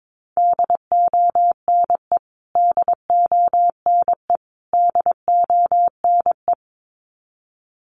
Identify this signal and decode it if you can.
Morse code, heard as a keyed tone, about 22 wpm, 700 Hz. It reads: DONE DONE DONE
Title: Top 500 Words Repeated 22wpm